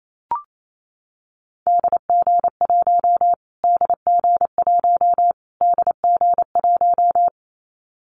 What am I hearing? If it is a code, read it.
Morse code, heard as a keyed tone, about 28 wpm, 700 Hz. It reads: DG1 DG1 DG1
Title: Callsign Prefixes for Top 25 DXCC - Repeated 3x - 28wpm